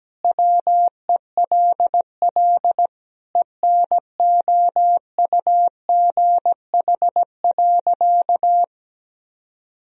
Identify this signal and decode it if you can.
Morse code, heard as a keyed tone, about 17 wpm, 700 Hz. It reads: WELL ENOUGH.
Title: Sentences from Top 400 Words and Abbreviations 17wpm